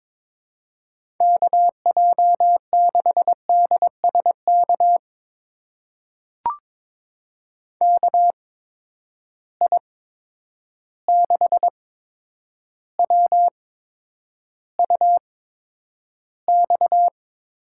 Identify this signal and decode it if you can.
Morse code, heard as a keyed tone, about 23 wpm, spaced as for 10 wpm, 700 Hz. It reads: KJ6DSK KI6WUX
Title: USA Call Signs - 6 Characters - 8x Character Spacing - 22wpm